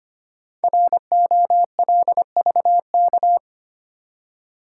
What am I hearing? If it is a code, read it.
Morse code, heard as a keyed tone, about 25 wpm, 700 Hz. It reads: ROLVK
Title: Code Groups - 5 Letter:Numbers - 25wpm